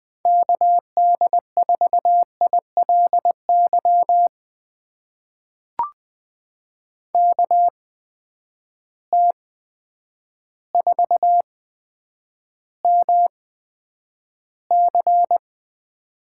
Morse code, heard as a keyed tone, about 20 wpm, spaced as for 9 wpm, 700 Hz. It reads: KD4ILY KT4MC
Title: USA Call Signs - 8x Character Spacing - 20wpm